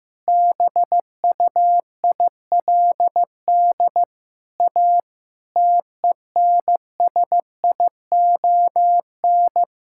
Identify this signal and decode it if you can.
Morse code, heard as a keyed tone, about 16 wpm, 700 Hz. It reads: BUILD A TENSION
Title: Sentences from Top 1600 Words 15wpm